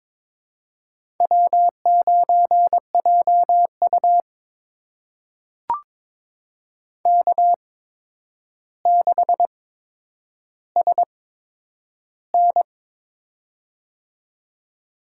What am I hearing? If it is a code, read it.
Morse code, heard as a keyed tone, about 22 wpm, spaced as for 9 wpm, 700 Hz. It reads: W9JU K6SN
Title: USA Call Signs - 4 Characters - 8x Character Spacing - 22wpm